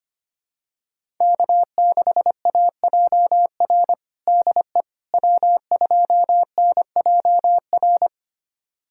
Morse code, heard as a keyed tone, about 25 wpm, 700 Hz. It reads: K6AJR DE W2NJR
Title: USA Callsign DE Callsign - 5 Characters - 25wpm